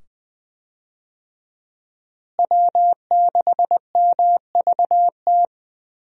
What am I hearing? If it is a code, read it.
Morse code, heard as a keyed tone, about 20 wpm, 700 Hz. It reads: W6MVT
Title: USA Call Signs - 5 Characters - Repeated 2x - 20wpm